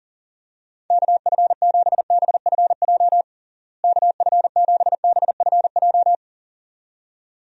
Morse code, heard as a keyed tone, about 40 wpm, 700 Hz. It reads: KF7BFJ KF7BFJ
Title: USA Call Signs - 6 Characters - Repeated 2x - 40wpm